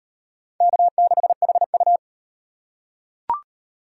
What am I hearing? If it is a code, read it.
Morse code, heard as a keyed tone, about 38 wpm, 700 Hz. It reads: K6HU